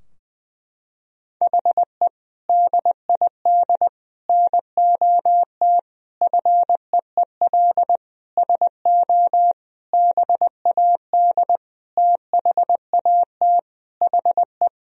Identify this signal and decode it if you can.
Morse code, heard as a keyed tone, about 20 wpm, 700 Hz. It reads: HE DID NOT FEEL SO BAD THAT HE
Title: Sentences from Top 1000 Words - Review Part 4 - 20wpm